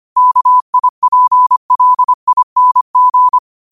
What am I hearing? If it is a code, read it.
Morse code, heard as a keyed tone, about 25 wpm, 1000 Hz. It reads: KIPLING
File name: qotdfiles / 25wpm